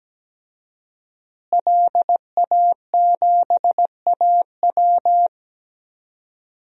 Morse code, heard as a keyed tone, about 17 wpm, 700 Hz. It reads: LA7AW
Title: DXCC-100 Call Signs - 5 Characters - Repeated 3x - 17wpm